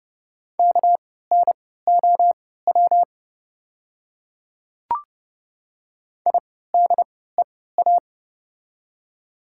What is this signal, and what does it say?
Morse code, heard as a keyed tone, about 30 wpm, spaced as for 17 wpm, 700 Hz. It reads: KNOW IDEA
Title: 3x Character Spacing - Top 300 Words - 30wpm